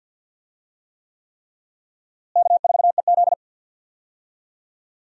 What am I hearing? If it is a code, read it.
Morse code, heard as a keyed tone, about 46 wpm, 700 Hz. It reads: K4EZ
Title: USA Call Signs - 4 Characters - Repeated 3x - 50wpm